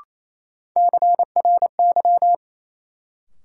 Morse code, heard as a keyed tone, about 28 wpm, 700 Hz. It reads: CRY
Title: Sentences from Top 500 Words - New Word Drill - 28wpm